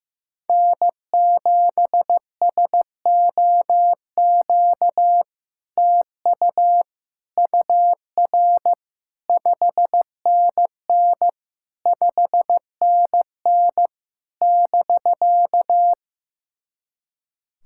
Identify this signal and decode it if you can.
Morse code, heard as a keyed tone, about 15 wpm, 700 Hz. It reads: N7SOQ TU UR 5NN 5NN <BK>
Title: SOTA QSO Practice 15wpm